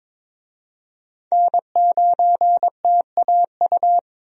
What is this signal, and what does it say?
Morse code, heard as a keyed tone, about 22 wpm, 700 Hz. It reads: N9TAU
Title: USA Call Signs - 5 Characters - Repeated 2x - 22wpm